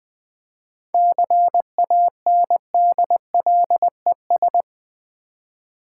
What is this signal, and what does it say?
Morse code, heard as a keyed tone, about 20 wpm, 700 Hz. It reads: CANDLES
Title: Easy Words with Hints - 20wpm